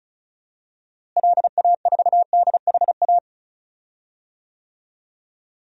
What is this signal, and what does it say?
Morse code, heard as a keyed tone, about 36 wpm, 700 Hz. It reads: LA4DHA
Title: DXCC-100 Call Signs - 6 Characters - Repeated 3x - 35wpm